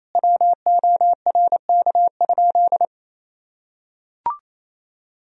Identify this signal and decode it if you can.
Morse code, heard as a keyed tone, about 28 wpm, 700 Hz. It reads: WORK?
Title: Sentences from Top 1000 Words - Review Part 4 - 28wpm